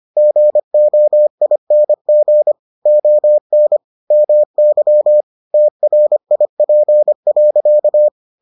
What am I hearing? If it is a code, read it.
Morse code, heard as a keyed tone, about 25 wpm, 600 Hz. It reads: GOING ON MY TRIP.